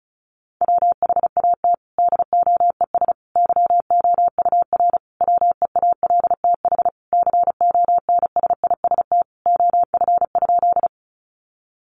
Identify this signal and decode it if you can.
Morse code, heard as a keyed tone, about 34 wpm, 700 Hz. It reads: WHAT DOES YOUR WEALTH CONSIST OF?
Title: Sentences from Top 1400 Words 35wpm